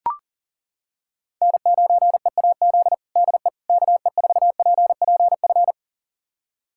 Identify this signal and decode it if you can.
Morse code, heard as a keyed tone, about 40 wpm, 700 Hz. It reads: N9EAZ DE KE4PPF
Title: USA Callsign DE Callsign 40wpm